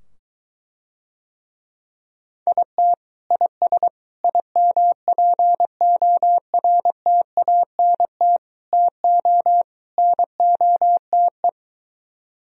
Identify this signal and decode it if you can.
Morse code, heard as a keyed tone, about 23 wpm, 700 Hz. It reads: IT IS IMPORTANT TO NOTE